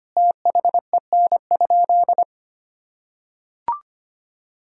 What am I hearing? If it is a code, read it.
Morse code, heard as a keyed tone, about 25 wpm, 700 Hz. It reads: THEN?
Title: Sentences from Top 1000 Words - Review Part 1 - 25wpm